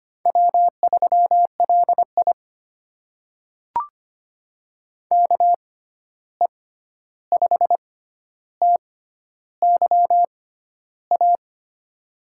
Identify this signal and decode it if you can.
Morse code, heard as a keyed tone, about 25 wpm, spaced as for 11 wpm, 700 Hz. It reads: W3LI KE5TYA
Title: USA Call Signs - 6x Character Spacing - 25wpm